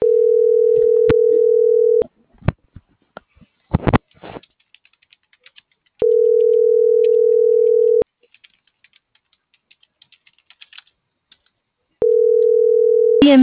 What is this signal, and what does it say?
Ringback tone (North America), 440 + 480 Hz.